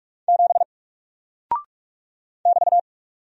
Morse code, heard as a keyed tone, about 44 wpm, 700 Hz. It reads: 7 <BT>
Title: Single Character (Letter:Number:Punctuation:Prosign) 45wpm